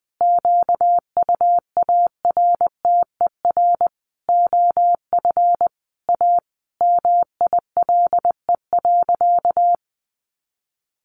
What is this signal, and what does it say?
Morse code, heard as a keyed tone, about 20 wpm, 700 Hz. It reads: QUARTER OF A MILE.
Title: Sentences from Top 1300 Words 20wpm